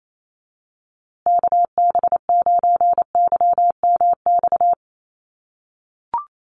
Morse code, heard as a keyed tone, about 28 wpm, 700 Hz. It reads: KB9YMX
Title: USA Call Signs Repeated 3x 28wpm